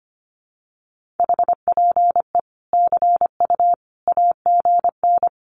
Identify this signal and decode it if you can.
Morse code, heard as a keyed tone, about 25 wpm, 700 Hz. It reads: HPE CU AGN
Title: Common QSO Phrases 25wpm